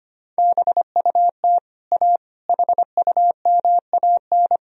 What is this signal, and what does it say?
Morse code, heard as a keyed tone, about 25 wpm, 700 Hz. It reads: BUT A HUMAN
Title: Sentences from Top 1000 Words - Review Part 3 - 25wpm